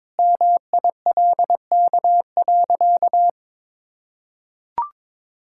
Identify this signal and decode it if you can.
Morse code, heard as a keyed tone, about 22 wpm, 690 Hz. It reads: MILK.
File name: practice / 22wpm / Easy Sentences with Hints - 22wpm